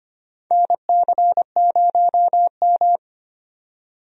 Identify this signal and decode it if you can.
Morse code, heard as a keyed tone, about 25 wpm, 700 Hz. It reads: NC0M
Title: USA Call Signs - 4 Characters - 25wpm